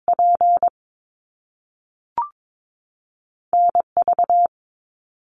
Morse code, heard as a keyed tone, about 22 wpm, 700 Hz. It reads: P NV
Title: Top Two Letter Combinations in Words 22wpm